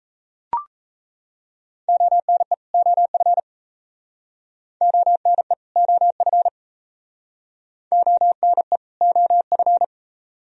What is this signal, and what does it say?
Morse code, heard as a keyed tone, about 38 wpm, 700 Hz. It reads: ONE OF ONE OF ONE OF